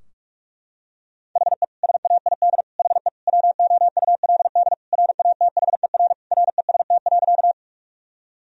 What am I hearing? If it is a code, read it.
Morse code, heard as a keyed tone, about 41 wpm, 700 Hz. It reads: HE SAID HE WOULD RATHER REST.
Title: Sentences from Top 1000 Words - Review Part 4 - 45wpm